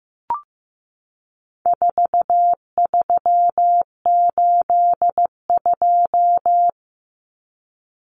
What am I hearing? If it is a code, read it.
Morse code, heard as a keyed tone, about 15 wpm, 700 Hz. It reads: 4382
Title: Common Cut Numbers - 4-Digit 15wpm